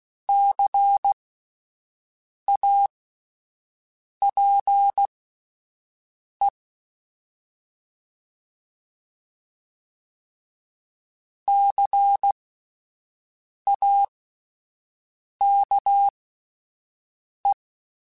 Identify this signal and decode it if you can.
Morse code, heard as a keyed tone, about 16 wpm, spaced as for 5 wpm, 800 Hz. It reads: CAPE CAKE
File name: Spaced Alphas / Spaced Alphas mp3s / Cs sp @10